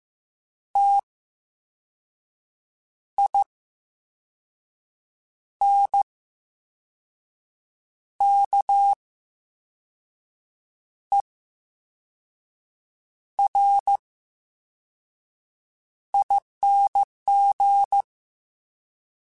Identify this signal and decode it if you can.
Morse code, heard as a keyed tone, about 15 wpm, spaced as for 5 wpm, 800 Hz. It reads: TINKERING
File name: Special INGs / Special INGs-mp3s / Slow ing @13